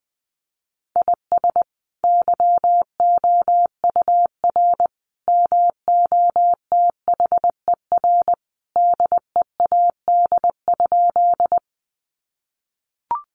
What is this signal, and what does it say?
Morse code, heard as a keyed tone, about 20 wpm, 700 Hz. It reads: IS YOUR MOTHER DEAD?
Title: Sentences from Top 1500 Words 20wpm